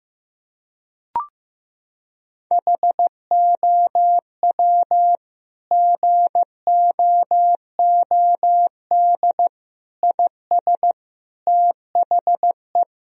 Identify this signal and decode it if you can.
Morse code, heard as a keyed tone, about 15 wpm, 700 Hz. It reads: HOW GOOD IS THE
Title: Sentences from Top 1000 Words - Review Part 4 - 15wpm